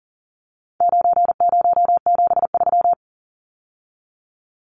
Morse code, heard as a keyed tone, about 37 wpm, 700 Hz. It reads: 9073
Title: Numbers - 4-Digit 40wpm